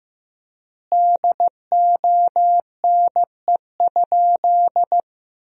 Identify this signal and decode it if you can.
Morse code, heard as a keyed tone, about 16 wpm, 700 Hz. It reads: DONE?
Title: 4x Word-Spacing - Sentences from Top 500 Words - 15wpm